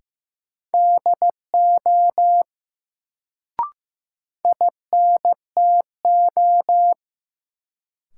Morse code, heard as a keyed tone, about 15 wpm, 700 Hz. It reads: DO INTO
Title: Sets of 2 Words - New Word Drill - 15wpm